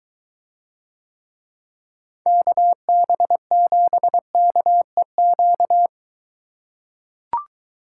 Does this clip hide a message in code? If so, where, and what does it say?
Yes: Morse code from 2.3 to 5.9 s, heard as a keyed tone, about 23 wpm, 700 Hz. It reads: KB7KEQ